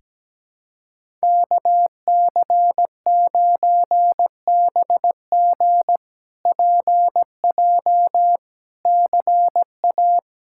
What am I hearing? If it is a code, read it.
Morse code, heard as a keyed tone, about 17 wpm, 700 Hz. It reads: KC9BG PJ CA
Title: K1USN SST QSO Practice - 17wpm